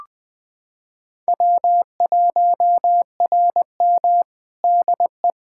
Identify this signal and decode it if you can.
Morse code, heard as a keyed tone, about 19 wpm, 700 Hz. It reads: W1RM DE
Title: Practice QSO - Exchange 4 - Ending - 20wpm